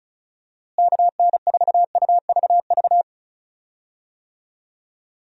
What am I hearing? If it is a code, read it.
Morse code, heard as a keyed tone, about 36 wpm, 700 Hz. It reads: KN4UVV
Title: USA Call Signs - 6 Characters - Repeated 2x - 35wpm